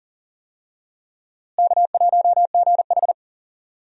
Morse code, heard as a keyed tone, about 40 wpm, 700 Hz. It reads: K1GH
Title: USA Call Signs - 4 Characters - Repeated 3x - 40wpm